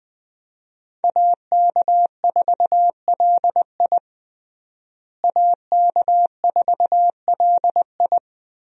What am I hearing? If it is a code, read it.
Morse code, heard as a keyed tone, about 21 wpm, 700 Hz. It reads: AK4LI AK4LI
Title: USA Call Signs - 5 Characters - Repeated 3x - 20wpm